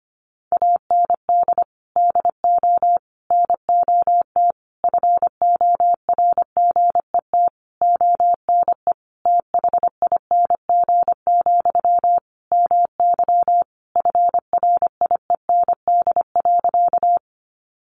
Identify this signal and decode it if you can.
Morse code, heard as a keyed tone, about 25 wpm, 700 Hz. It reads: AND DO NOT FORGET ONE THING, MY FRIEND.
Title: Sentences from Top 1200 Words 25wpm